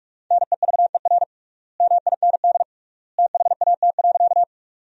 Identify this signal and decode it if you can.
Morse code, heard as a keyed tone, about 46 wpm, 700 Hz. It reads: NEVER MIND THAT.
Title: 3x Word-Spacing - Sentences from Top 500 Words - 45wpm